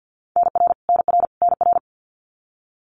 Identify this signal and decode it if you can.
Morse code, heard as a keyed tone, about 50 wpm, 700 Hz. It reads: NR NR NR
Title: Abbreviations Repeated 3x - 50wpm